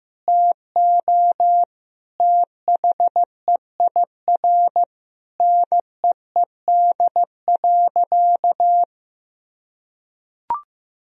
Morse code, heard as a keyed tone, about 15 wpm, 700 Hz. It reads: TO THEIR NEED.
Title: Sentences from Top 1000 Words - Review Part 4 - 15wpm